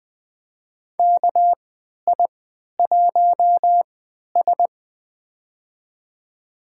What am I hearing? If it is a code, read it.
Morse code, heard as a keyed tone, about 20 wpm, spaced as for 11 wpm, 700 Hz. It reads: KI1S
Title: USA Call Signs - 4 Characters - 3x Character Spacing - 20wpm